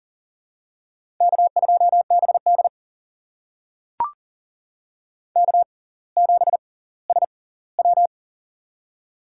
Morse code, heard as a keyed tone, about 40 wpm, spaced as for 20 wpm, 700 Hz. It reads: K2BD K7SW
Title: USA Call Signs - 6x Character Spacing - 40wpm